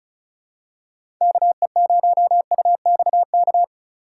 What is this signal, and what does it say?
Morse code, heard as a keyed tone, about 36 wpm, 700 Hz. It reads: KE0UXK
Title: USA Call Signs - 6 Characters - 8x Character Spacing - 35wpm